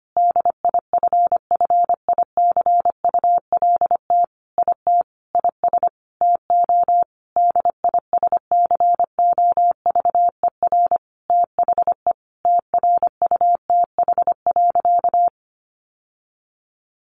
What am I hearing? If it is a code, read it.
Morse code, heard as a keyed tone, about 25 wpm, 700 Hz. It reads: DIFFICULT IT IS TO DISCOVER THE TRUTH.
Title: Sentences from Top 1100 Words 25wpm